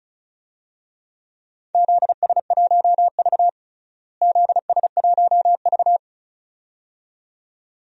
Morse code, heard as a keyed tone, about 36 wpm, 700 Hz. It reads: ZS1V ZS1V
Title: DXCC-100 Call Signs - 4 Characters - Repeated 2x - 35wpm